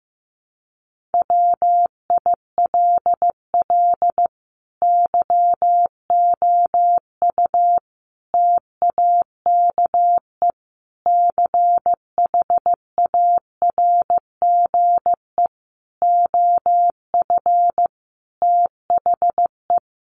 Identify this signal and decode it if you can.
Morse code, heard as a keyed tone, about 14 wpm, 700 Hz. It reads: WILL YOU TAKE CHARGE OF THE
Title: Sentences from Top 900 Words 15wpm